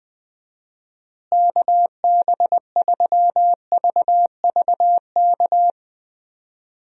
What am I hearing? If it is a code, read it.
Morse code, heard as a keyed tone, about 20 wpm, 700 Hz. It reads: KB3VVK
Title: USA Call Signs - 6 Characters - Repeated 3x - 20wpm